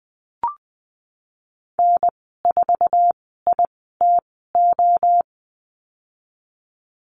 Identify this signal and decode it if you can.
Morse code, heard as a keyed tone, about 20 wpm, spaced as for 15 wpm, 700 Hz. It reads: N4ITO
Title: 048 Lesson 9g - Number 4 - Call Signs 20wpm